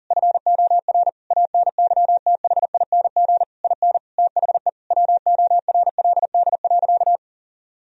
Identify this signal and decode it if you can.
Morse code, heard as a keyed tone, about 40 wpm, 700 Hz. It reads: FOR ANYTHING IN THE WORLD.
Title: Sentences from Top 1000 Words - Review Part 3 - 40wpm